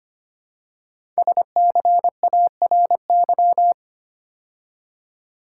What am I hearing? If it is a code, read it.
Morse code, heard as a keyed tone, about 25 wpm, 700 Hz. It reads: SCARY
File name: 5 Letter Words - First 3 Letters Given - 25wpm